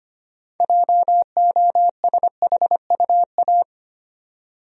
Morse code, heard as a keyed tone, about 25 wpm, 700 Hz. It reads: JOSHUA
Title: USA Common Names - 25wpm